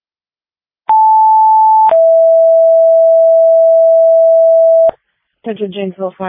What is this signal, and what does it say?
Two-tone page, heard as a one-second tone, then a three-second tone, about 880 then 660 Hz, with speech. It is heard from 0.9 to 4.9 s.